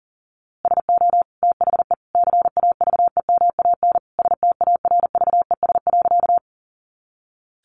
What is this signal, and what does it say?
Morse code, heard as a keyed tone, about 40 wpm, 700 Hz. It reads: SO THE CAVEMAN STARVES.
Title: A Sound Of Thunder 40wpm